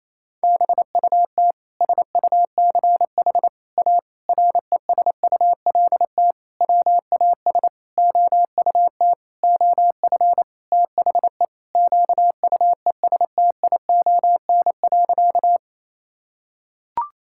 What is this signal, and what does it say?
Morse code, heard as a keyed tone, about 28 wpm, 700 Hz. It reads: BUT SUCH A RESULT WAS OUT OF THE QUESTION.
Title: Sentences from Top 1000 Words - Review Part 5 - 28wpm